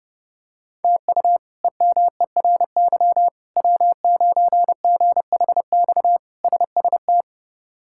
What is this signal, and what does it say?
Morse code, heard as a keyed tone, about 32 wpm, 700 Hz. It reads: TU EMERY W9GHX SST
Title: K1USN SST QSO Practice - 30wpm